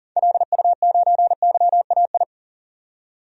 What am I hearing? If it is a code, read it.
Morse code, heard as a keyed tone, about 40 wpm, 700 Hz. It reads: LU9YAI
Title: DXCC-100 Call Signs - 6 Characters - 40wpm